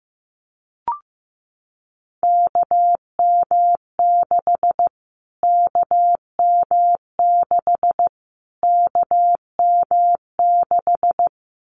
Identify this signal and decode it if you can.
Morse code, heard as a keyed tone, about 15 wpm, 700 Hz. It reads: KM6 KM6 KM6
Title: Callsign Prefixes for Top 25 DXCC - Repeated 3x - 15wpm